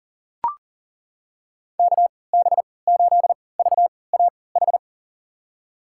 Morse code, heard as a keyed tone, about 40 wpm, spaced as for 23 wpm, 700 Hz. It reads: KB8VAH
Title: USA Call Signs - 6 Characters - 3x Character Spacing - 40wpm